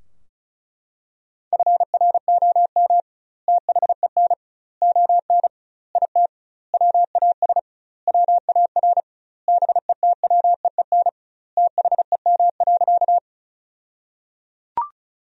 Morse code, heard as a keyed tone, about 36 wpm, 700 Hz. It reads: FROM THEN ON IT WAS WAR BETWEEN THEM.
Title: 2x Word-Spacing - Sentences from Top 400 Words - 35wpm